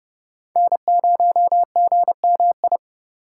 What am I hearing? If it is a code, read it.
Morse code, heard as a keyed tone, about 30 wpm, 700 Hz. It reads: N0GMI
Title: USA Call Signs - 6x Character Spacing - 30wpm